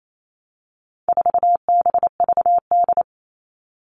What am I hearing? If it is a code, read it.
Morse code, heard as a keyed tone, about 28 wpm, 700 Hz. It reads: 4BVD
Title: All Cut Numbers - 4-Digit 28wpm